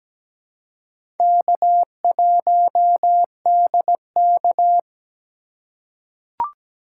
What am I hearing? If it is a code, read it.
Morse code, heard as a keyed tone, about 17 wpm, 700 Hz. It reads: K1DK
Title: USA Call Signs - 4 Characters - 2x Character Spacing - 17wpm